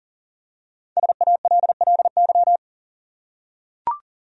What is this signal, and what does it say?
Morse code, heard as a keyed tone, about 40 wpm, 700 Hz. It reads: SALLY
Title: USA Common Names - Max 5 Letters - 40wpm